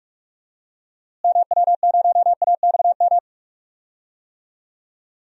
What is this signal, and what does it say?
Morse code, heard as a keyed tone, about 42 wpm, 700 Hz. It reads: MW0AXM
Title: DXCC-100 Call Signs - 6 Characters - Repeated 2x - 45wpm